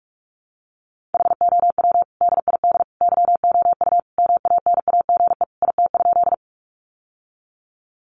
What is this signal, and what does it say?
Morse code, heard as a keyed tone, about 44 wpm, 700 Hz. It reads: HOW DID YOU MANAGE IT?
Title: Sentences from Top 1400 Words 45wpm